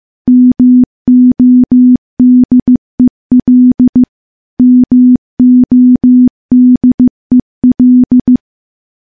Morse code, heard as a keyed tone, about 15 wpm, 250 Hz. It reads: MODEL MODEL
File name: morse-training-15WPM